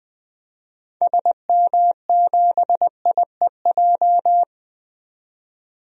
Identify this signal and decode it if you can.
Morse code, heard as a keyed tone, about 21 wpm, 700 Hz. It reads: SM7IEJ
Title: DXCC-100 Call Signs - 6 Characters - 20wpm